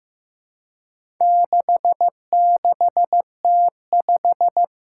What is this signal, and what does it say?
Morse code, heard as a keyed tone, about 15 wpm, 700 Hz. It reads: 66T5
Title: Four Challenging Letter-Number Combinations 15wpm